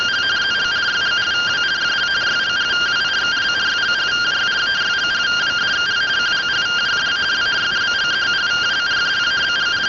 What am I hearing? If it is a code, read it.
Two-tone teleprinter signal, about 170 Hz shift.